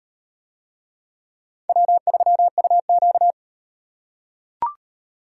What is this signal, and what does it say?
Morse code, heard as a keyed tone, about 35 wpm, 700 Hz. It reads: W3UQ